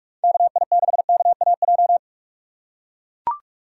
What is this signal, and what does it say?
Morse code, heard as a keyed tone, about 42 wpm, 700 Hz. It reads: KI6KAJ
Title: USA Call Signs - 3x Character Spacing - 45wpm